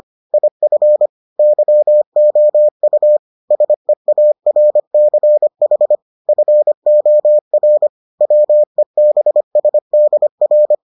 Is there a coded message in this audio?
Yes: Morse code from 0.3 to 10.7 s, heard as a keyed tone, about 25 wpm, 600 Hz. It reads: IF YOU SEARCH FOR WEBSDR